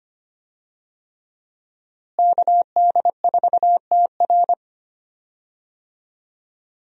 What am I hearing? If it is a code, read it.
Morse code, heard as a keyed tone, about 25 wpm, 700 Hz. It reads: KD4TR
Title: USA Call Signs - 5 Characters - Repeated 3x - 25wpm